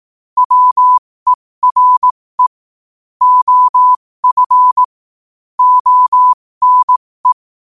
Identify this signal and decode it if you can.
Morse code, heard as a keyed tone, about 18 wpm, spaced as for 15 wpm, 1000 Hz. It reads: WERE OF ONE
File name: qotdfiles / 15wpm